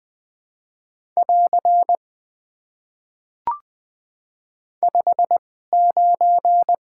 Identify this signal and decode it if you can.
Morse code, heard as a keyed tone, about 21 wpm, spaced as for 15 wpm, 700 Hz. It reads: <AR> 59
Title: Top 90 QSO Elements - 2x Character Spacing - 20wpm